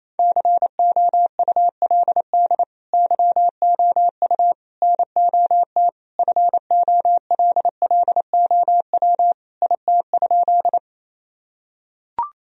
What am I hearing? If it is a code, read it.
Morse code, heard as a keyed tone, about 28 wpm, 700 Hz. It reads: COULD YOU NOT FOLLOW IT?
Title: Sentences from Top 1000 Words - Review Part 1 - 28wpm